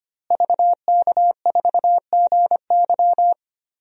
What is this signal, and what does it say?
Morse code, heard as a keyed tone, about 25 wpm, 700 Hz. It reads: VK4GY
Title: DXCC-100 Call Signs - 5 Characters - Repeated 2x - 25wpm